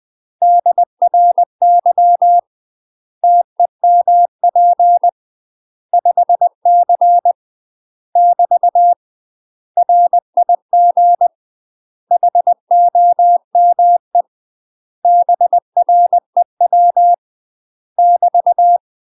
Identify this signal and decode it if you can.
Morse code, heard as a keyed tone, about 20 wpm, 700 Hz. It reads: DRY TEMP 5C <BT> RIG HOME BREW <BT>
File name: UK MASTER QSOs TRAINING / 20WPM / UK MASTER QSOs 0061